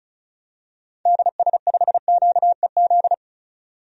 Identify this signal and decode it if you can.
Morse code, heard as a keyed tone, about 36 wpm, 700 Hz. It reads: DS5QEZ
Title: DXCC-100 Call Signs - 6 Characters - 35wpm